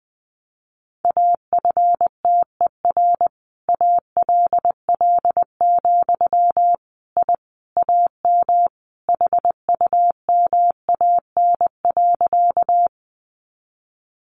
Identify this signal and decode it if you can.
Morse code, heard as a keyed tone, about 20 wpm, 700 Hz. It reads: AFTER ALL, I AM HUMAN.
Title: Sentences from Top 600 Words 20wpm